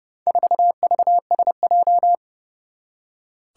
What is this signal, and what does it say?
Morse code, heard as a keyed tone, about 29 wpm, 700 Hz. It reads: 4VSJ
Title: Four Challenging Letter-Number Combinations 30wpm